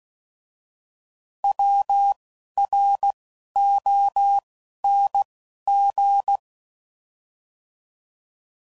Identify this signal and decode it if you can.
Morse code, heard as a keyed tone, about 16 wpm, spaced as for 11 wpm, 800 Hz. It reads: WRONG
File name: Mixed / Misc / Words